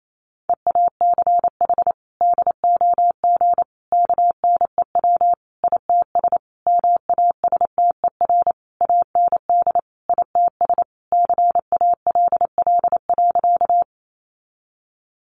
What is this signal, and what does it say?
Morse code, heard as a keyed tone, about 28 wpm, 700 Hz. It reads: EACH DOG KNEW ITS MASTER AND ITS CALL.
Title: Sentences from Top 1300 Words 28wpm